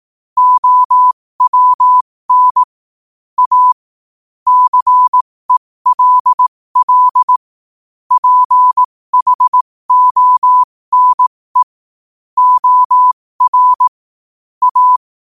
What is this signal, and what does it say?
Morse code, heard as a keyed tone, about 18 wpm, spaced as for 15 wpm, 1000 Hz. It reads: OWN A CELL PHONE OR A